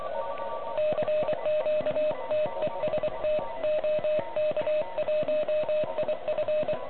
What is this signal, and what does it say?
Morse code, heard as a keyed tone, about 24 wpm, 620 Hz, with noise. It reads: CQ TEST OK1IF
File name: ARCHIV / OK QRP 2021 / REC05